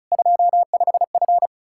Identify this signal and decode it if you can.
Morse code, heard as a keyed tone, about 38 wpm, 700 Hz. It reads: 25F